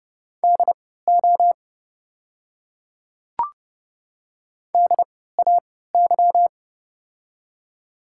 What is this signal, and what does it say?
Morse code, heard as a keyed tone, about 32 wpm, spaced as for 17 wpm, 700 Hz. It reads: DO DAY
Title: 3x Character Spacing - Top 500 Words - 30wpm